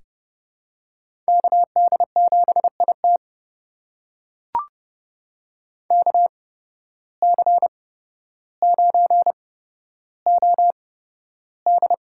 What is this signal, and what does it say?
Morse code, heard as a keyed tone, about 30 wpm, spaced as for 13 wpm, 700 Hz. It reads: KD7IT KC9OD
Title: USA Call Signs - 5 Characters - 8x Character Spacing - 30wpm